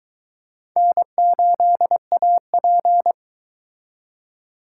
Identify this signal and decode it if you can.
Morse code, heard as a keyed tone, about 22 wpm, 700 Hz. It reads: N8AP